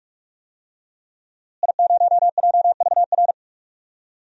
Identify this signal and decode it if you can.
Morse code, heard as a keyed tone, about 45 wpm, 700 Hz. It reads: I0JVR
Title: DXCC-100 Call Signs - 5 Characters - Repeated 3x - 45wpm